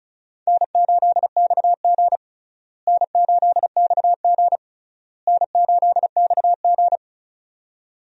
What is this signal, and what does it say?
Morse code, heard as a keyed tone, about 34 wpm, 700 Hz. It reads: N8XG N8XG N8XG
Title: USA Call Signs - 4 Characters - Repeated 3x - 35wpm